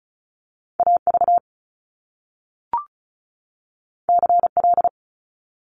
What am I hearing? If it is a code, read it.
Morse code, heard as a keyed tone, about 36 wpm, 700 Hz. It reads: AV CL
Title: Top Two Letter Combinations in Words 35wpm